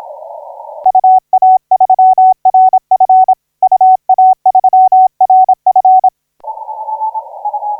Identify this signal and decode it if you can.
Morse code, heard as a keyed tone, about 26 wpm, 750 Hz. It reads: UA3RF UA3RF
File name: 9G5AA 1712UTC 12m